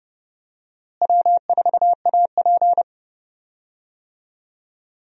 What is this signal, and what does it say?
Morse code, heard as a keyed tone, about 28 wpm, 700 Hz. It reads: W4AP
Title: USA Call Signs - 4 Characters - Repeated 3x - 30wpm